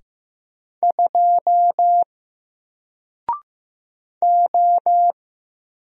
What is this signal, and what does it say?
Morse code, heard as a keyed tone, about 15 wpm, 700 Hz. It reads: 2 O
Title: Single Letter-Number Alt 15wpm